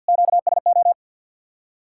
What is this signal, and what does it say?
Morse code, heard as a keyed tone, about 65 wpm, 700 Hz. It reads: QSO